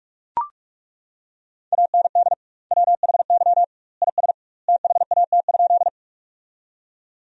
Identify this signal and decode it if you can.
Morse code, heard as a keyed tone, about 41 wpm, 700 Hz. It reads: AND WHY IS THAT?
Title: 2x Word-Spacing - Sentences from Top 200 Words - 45wpm